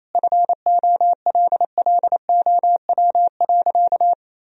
Morse code, heard as a keyed tone, about 28 wpm, 700 Hz. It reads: FOLLOW.
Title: Sentences from Top 1000 Words - Review Part 3 - 28wpm